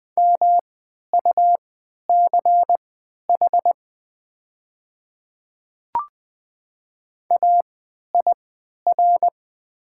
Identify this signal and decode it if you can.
Morse code, heard as a keyed tone, about 20 wpm, spaced as for 11 wpm, 700 Hz. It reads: MUCH AIR
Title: 3x Character Spacing - Top 200 Words - 20wpm